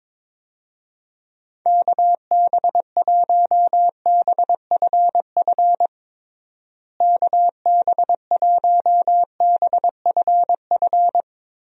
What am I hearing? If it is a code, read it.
Morse code, heard as a keyed tone, about 22 wpm, 700 Hz. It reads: KB1BFF KB1BFF
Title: USA Call Signs - 6 Characters - Repeated 3x - 22wpm